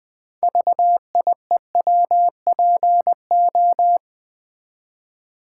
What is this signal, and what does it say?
Morse code, heard as a keyed tone, about 20 wpm, 700 Hz. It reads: VIEWPO
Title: Build A Word - 3 Letters At A Time - 9-Letter Words - 20wpm